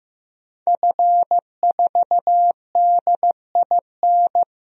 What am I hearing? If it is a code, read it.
Morse code, heard as a keyed tone, about 15 wpm, 700 Hz. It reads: F4DIN
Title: DXCC-100 Call Signs - 5 Characters - 15wpm